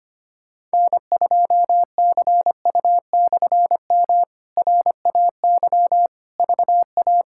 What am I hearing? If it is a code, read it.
Morse code, heard as a keyed tone, about 25 wpm, 700 Hz. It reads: N2CU/M RAY VA
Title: K1USN SST QSO Practice - 25wpm